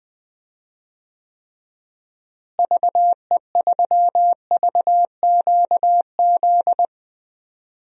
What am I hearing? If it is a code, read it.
Morse code, heard as a keyed tone, about 21 wpm, 700 Hz. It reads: VE3VQZ
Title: DXCC-100 Call Signs - 6 Characters - Repeated 2x - 20wpm